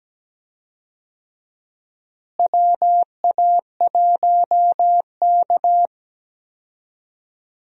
Morse code, heard as a keyed tone, about 17 wpm, 700 Hz. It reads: WA1K
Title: USA Call Signs - 4 Characters - Repeated 3x - 17wpm